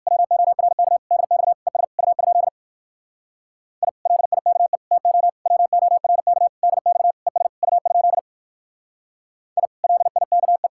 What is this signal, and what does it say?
Morse code, heard as a keyed tone, about 55 wpm, 700 Hz. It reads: WORK DX ES U? I LIKE TO WORK DX ES U? I LIKE
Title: Speed-Racing Common QSO Phrases - 45wpm